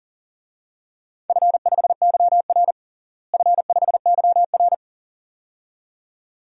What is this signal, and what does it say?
Morse code, heard as a keyed tone, about 38 wpm, 700 Hz. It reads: F5YR F5YR
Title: DXCC-100 Call Signs - 4 Characters - Repeated 2x - 40wpm